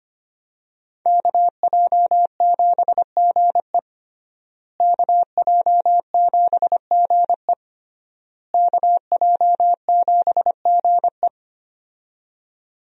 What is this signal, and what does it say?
Morse code, heard as a keyed tone, about 25 wpm, 700 Hz. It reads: KJ7GE KJ7GE KJ7GE
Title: USA Call Signs - 5 Characters - Repeated 3x - 25wpm